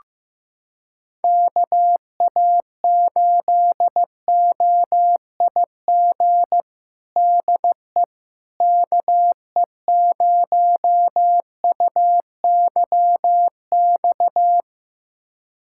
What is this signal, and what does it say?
Morse code, heard as a keyed tone, about 15 wpm, 700 Hz. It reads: KA8OIG DE KE0UYX
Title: USA Callsign DE Callsign - 6 Characters - 15wpm